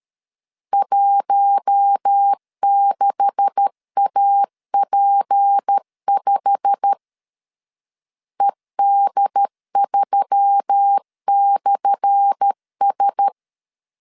Morse code, heard as a keyed tone, about 13 wpm, 800 Hz. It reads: 16AP5 ED3/S
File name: LEZIONI / 10 SIGNS / 10A SLASH 12wpm 2314